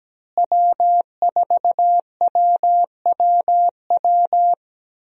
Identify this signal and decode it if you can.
Morse code, heard as a keyed tone, about 17 wpm, 700 Hz. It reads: W4WWW
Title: USA Call Signs - 5 Characters - Repeated 2x - 17wpm